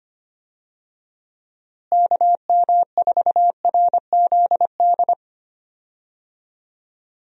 Morse code, heard as a keyed tone, about 25 wpm, 700 Hz. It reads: KM4RZD
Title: USA Call Signs - 6 Characters - Repeated 2x - 25wpm